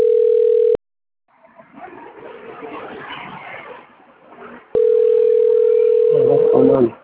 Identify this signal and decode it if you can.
Ringback tone (North America), 440 + 480 Hz.